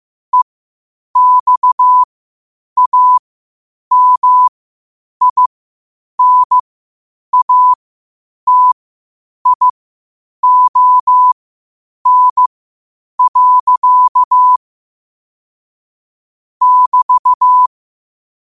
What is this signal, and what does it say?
Morse code, heard as a keyed tone, about 15 wpm, spaced as for 8 wpm, 1000 Hz. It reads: EXAMINATION. <BT>
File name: qotdfiles / 8wpm